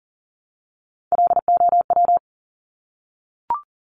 Morse code, heard as a keyed tone, about 38 wpm, 700 Hz. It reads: LOW
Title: Top Three Letter Combinations in Words 40wpm